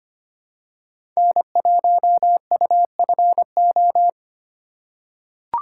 Morse code, heard as a keyed tone, about 25 wpm, 700 Hz. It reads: N1UFO
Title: USA Call Signs - 3x Character Spacing - 25wpm